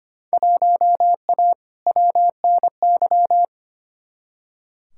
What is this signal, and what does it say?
Morse code, heard as a keyed tone, about 25 wpm, 700 Hz. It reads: 1A WNY
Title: Field Day Exchanges Encore - 25wpm